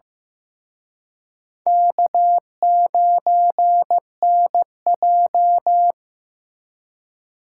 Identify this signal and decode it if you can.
Morse code, heard as a keyed tone, about 18 wpm, 700 Hz. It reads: K9NJ
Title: Speed Racing - USA 1x2 Call Signs 15wpm